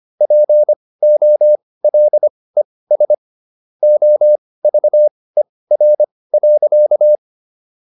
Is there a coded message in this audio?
Yes: Morse code from 0.2 to 7.2 s, heard as a keyed tone, about 25 wpm, spaced as for 18 wpm, 600 Hz. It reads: POLES OVER.